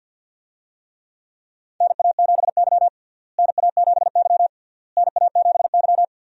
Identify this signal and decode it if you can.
Morse code, heard as a keyed tone, about 48 wpm, 700 Hz. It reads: NA7Y NA7Y NA7Y
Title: USA Call Signs - 4 Characters - Repeated 3x - 50wpm